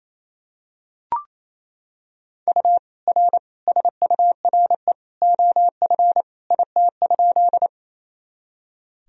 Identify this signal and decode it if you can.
Morse code, heard as a keyed tone, about 28 wpm, 700 Hz. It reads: U R SURE OF IT?
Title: Sentences from Top 300 Words and Abbreviations 28wpm